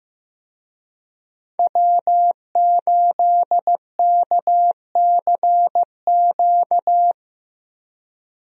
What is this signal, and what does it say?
Morse code, heard as a keyed tone, about 15 wpm, 700 Hz. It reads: W8KCQ
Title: USA Call Signs - 5 Characters - Repeated 2x - 15wpm